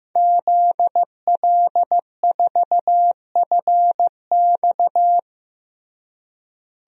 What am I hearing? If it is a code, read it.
Morse code, heard as a keyed tone, about 16 wpm, 700 Hz. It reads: ZL4FX
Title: DXCC-100 Call Signs - 5 Characters - Repeated 3x - 15wpm